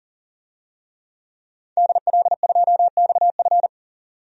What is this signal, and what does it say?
Morse code, heard as a keyed tone, about 40 wpm, 700 Hz. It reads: DL2XF
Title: DXCC-100 Call Signs - 5 Characters - 40wpm